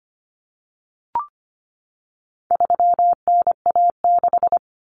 Morse code, heard as a keyed tone, about 25 wpm, 700 Hz. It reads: 3NA6
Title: Common Cut Numbers - 4-Digit 25wpm